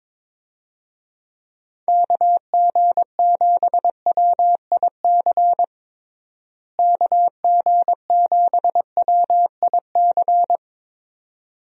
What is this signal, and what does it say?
Morse code, heard as a keyed tone, about 21 wpm, 700 Hz. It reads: KG7WIC KG7WIC
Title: USA Call Signs - 6 Characters - Repeated 2x - 22wpm